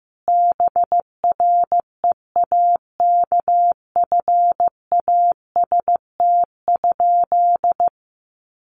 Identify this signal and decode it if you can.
Morse code, heard as a keyed tone, about 15 wpm, 700 Hz. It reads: BREAKFAST?
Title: Sentences from Top 1500 Words 15wpm